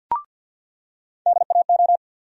Morse code, heard as a keyed tone, about 47 wpm, 700 Hz. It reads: DAO
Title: USA Call Sign Postfixes - 3 Letters 50wpm